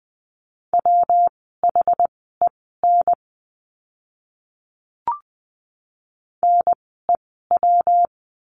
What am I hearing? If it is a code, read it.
Morse code, heard as a keyed tone, about 20 wpm, spaced as for 14 wpm, 700 Hz. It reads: WHEN NEW
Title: 146 Lesson 24f - Letter y - Words 20wpm